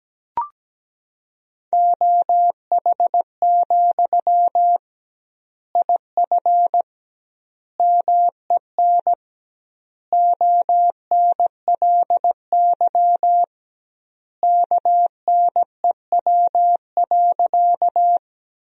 Morse code, heard as a keyed tone, about 17 wpm, 700 Hz. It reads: OH, IF MEN ONLY KNEW.
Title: 2x Word-Spacing - Sentences from Top 500 Words - 17wpm